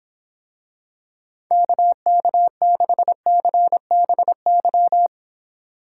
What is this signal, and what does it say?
Morse code, heard as a keyed tone, about 26 wpm, 700 Hz. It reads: KK6CBY